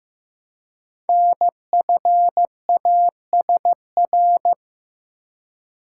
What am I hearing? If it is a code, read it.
Morse code, heard as a keyed tone, about 15 wpm, 700 Hz. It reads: NFASR
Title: Code Groups - 5 Letters - 15wpm